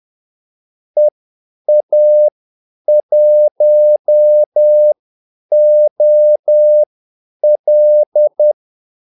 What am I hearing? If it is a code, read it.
Morse code, heard as a keyed tone, about 10 wpm, spaced as for 8 wpm, 600 Hz. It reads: EA1OL
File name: Callsigns-10-8wpm